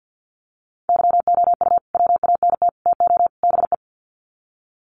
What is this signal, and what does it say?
Morse code, heard as a keyed tone, about 50 wpm, 700 Hz. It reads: YOU WANT TO BE
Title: Sets of 4 Words 50wpm